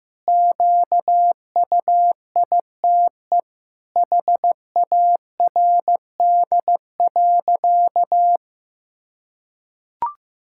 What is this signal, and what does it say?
Morse code, heard as a keyed tone, about 15 wpm, 700 Hz. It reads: QUITE HARD.
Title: Pangram Challenge v2 - 15wpm